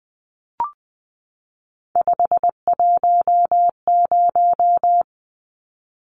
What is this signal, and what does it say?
Morse code, heard as a keyed tone, about 20 wpm, 700 Hz. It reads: 510
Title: Numbers - 3-Digit 20wpm